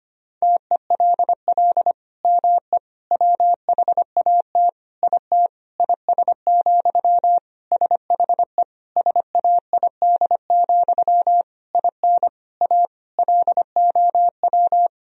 Morse code, heard as a keyed tone, about 25 wpm, 700 Hz. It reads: TELL ME WHAT IT IS, SHE SAID, IN A LOW
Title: Sentences from Top 1000 Words - Review Part 1 - 25wpm